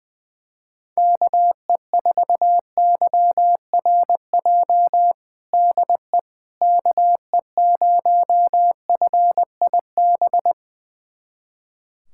Morse code, heard as a keyed tone, about 20 wpm, 700 Hz. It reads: KE4YRJ DE KE0FIB
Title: USA Callsign DE Callsign - 6 Characters - 20wpm